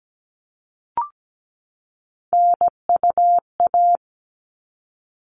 Morse code, heard as a keyed tone, about 17 wpm, 700 Hz. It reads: NUA
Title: All Cut Numbers - 3-Digit 17wpm